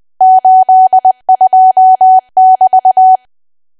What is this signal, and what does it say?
Morse code, heard as a keyed tone, about 20 wpm, 750 Hz. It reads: 82<BT>
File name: cwa / W1AW-Level3 / 20WPM